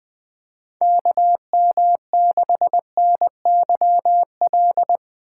Morse code, heard as a keyed tone, about 20 wpm, 700 Hz. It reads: KM6NYL
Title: USA Call Signs - 6 Characters - 3x Character Spacing - 20wpm